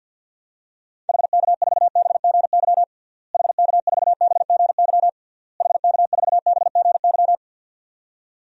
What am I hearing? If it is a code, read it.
Morse code, heard as a keyed tone, about 49 wpm, 700 Hz. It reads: HK4BGY HK4BGY HK4BGY
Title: DXCC-100 Call Signs - 6 Characters - Repeated 3x - 50wpm